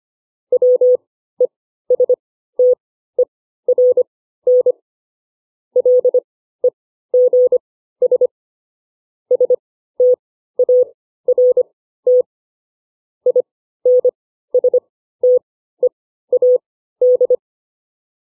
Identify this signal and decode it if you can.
Morse code, heard as a keyed tone, about 25 wpm, spaced as for 14 wpm, 500 Hz. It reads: WESTERN LEGS START INSTEAD
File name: CW / farnsworth timing